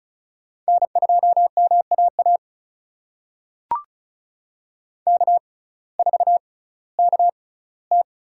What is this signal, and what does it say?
Morse code, heard as a keyed tone, about 36 wpm, spaced as for 19 wpm, 700 Hz. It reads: N2MAA K4KT
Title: USA Call Signs - 6x Character Spacing - 35wpm